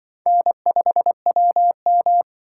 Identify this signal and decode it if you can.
Morse code, heard as a keyed tone, about 24 wpm, 700 Hz. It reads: N5WM
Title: Speed Racing - USA 1x2 Call Signs 20wpm